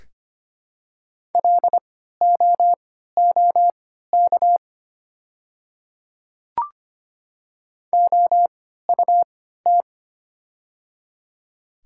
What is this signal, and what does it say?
Morse code, heard as a keyed tone, about 25 wpm, spaced as for 14 wpm, 700 Hz. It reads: LOOK OUT
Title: 3x Character Spacing - Top 200 Words - 25wpm